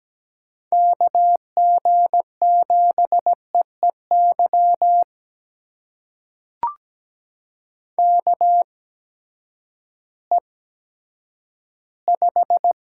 Morse code, heard as a keyed tone, about 17 wpm, spaced as for 10 wpm, 700 Hz. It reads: KG7EEY KE5
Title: USA Call Signs - 6 Characters - 8x Character Spacing - 17wpm